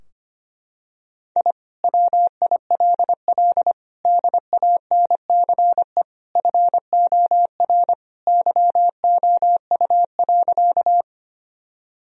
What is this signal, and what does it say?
Morse code, heard as a keyed tone, about 25 wpm, 690 Hz. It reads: I WILL DANCE FOR YOU.
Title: Sentences from Top 1000 Words - Review Part 4 - 25wpm